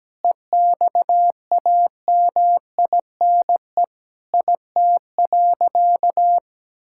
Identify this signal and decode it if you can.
Morse code, heard as a keyed tone, about 17 wpm, 700 Hz. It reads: EXAMINE IT.
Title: Sentences from Top 1600 Words 17wpm